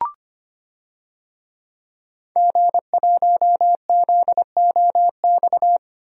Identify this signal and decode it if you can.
Morse code, heard as a keyed tone, about 25 wpm, 700 Hz. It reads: G1ZOX
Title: DXCC-100 Call Signs - 5 Characters - Repeated 3x - 25wpm